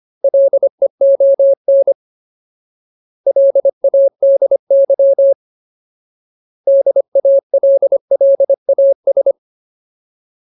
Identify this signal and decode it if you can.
Morse code, heard as a keyed tone, about 25 wpm, 550 Hz. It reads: LEON LADY DALLAS
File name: Ultimate Word List / 25WPM Ultimate